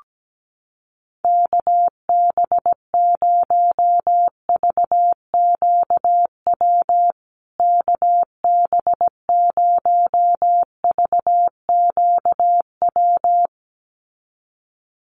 Morse code, heard as a keyed tone, about 17 wpm, 700 Hz. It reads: KB0VQW KB0VQW
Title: USA Call Signs Repeated 2x 17wpm